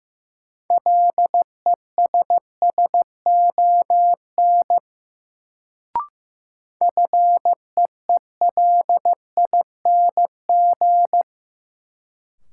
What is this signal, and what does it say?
Morse code, heard as a keyed tone, about 15 wpm, 700 Hz. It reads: LESSON FEELING
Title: Sentences from Top 1200 Words - New Word Drill Alt - 15wpm